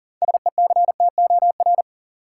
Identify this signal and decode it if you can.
Morse code, heard as a keyed tone, about 40 wpm, 700 Hz. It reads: SECTOR
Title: Six Letter Words - 40wpm